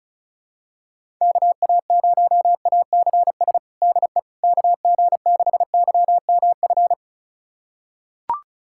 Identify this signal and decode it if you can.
Morse code, heard as a keyed tone, about 34 wpm, 700 Hz. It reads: KA0ACS DE KG6YMF
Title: USA Callsign DE Callsign 35wpm